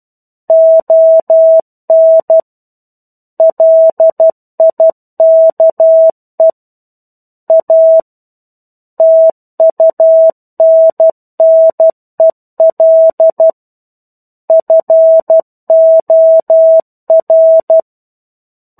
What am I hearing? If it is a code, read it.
Morse code, heard as a keyed tone, about 12 wpm, 650 Hz. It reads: ON LIKE A TUNNEL FOR
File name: 01-Chapter-01 12 WPM